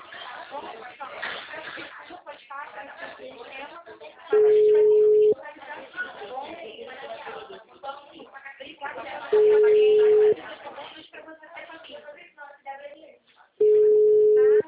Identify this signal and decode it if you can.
Ringback tone (Europe, 425 Hz), with speech.